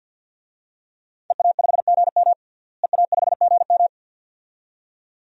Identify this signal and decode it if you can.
Morse code, heard as a keyed tone, about 52 wpm, 700 Hz. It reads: EA5GM EA5GM
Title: DXCC-100 Call Signs - 5 Characters - Repeated 2x - 50wpm